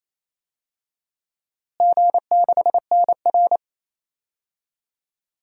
Morse code, heard as a keyed tone, about 28 wpm, 700 Hz. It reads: G6NR
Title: DXCC-100 Call Signs - 4 Characters - Repeated 3x - 28wpm